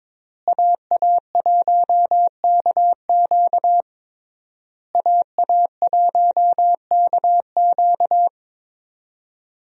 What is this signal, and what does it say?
Morse code, heard as a keyed tone, about 22 wpm, 700 Hz. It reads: AA1KQ AA1KQ
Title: USA Call Signs - 5 Characters - Repeated 3x - 22wpm